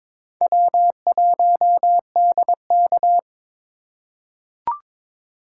Morse code, heard as a keyed tone, about 23 wpm, 700 Hz. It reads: W1DK
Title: USA Call Signs - 4 Characters - 3x Character Spacing - 22wpm